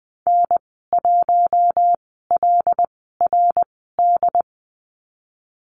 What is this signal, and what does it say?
Morse code, heard as a keyed tone, about 20 wpm, spaced as for 14 wpm, 700 Hz. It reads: N1LRD
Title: 082 Lesson 14g - Number 2 - Call Signs 20wpm